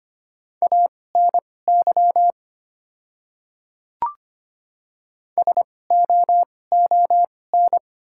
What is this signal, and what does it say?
Morse code, heard as a keyed tone, about 25 wpm, spaced as for 18 wpm, 700 Hz. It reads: ANY SOON
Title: 2x Character Spacing - Top 500 Words - 25wpm